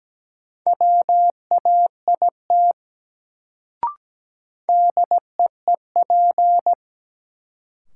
Morse code, heard as a keyed tone, about 17 wpm, 700 Hz. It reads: WAIT DEEP
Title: Sentences from Top 500 Words - New Word Drill - 17wpm